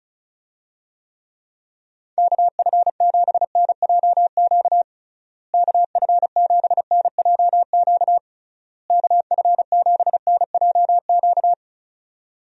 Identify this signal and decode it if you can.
Morse code, heard as a keyed tone, about 34 wpm, 700 Hz. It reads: KF7NJQ KF7NJQ KF7NJQ
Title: USA Call Signs - 6 Characters - Repeated 3x - 35wpm